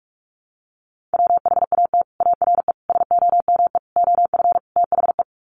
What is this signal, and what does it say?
Morse code, heard as a keyed tone, about 44 wpm, 700 Hz. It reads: WHAT ARE SOME OF THE
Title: Sets of 5 Words Encore 45wpm